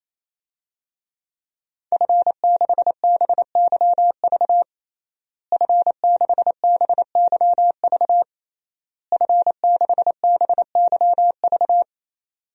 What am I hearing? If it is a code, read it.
Morse code, heard as a keyed tone, about 28 wpm, 700 Hz. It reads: F6BYV F6BYV F6BYV
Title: DXCC-100 Call Signs - 5 Characters - Repeated 3x - 28wpm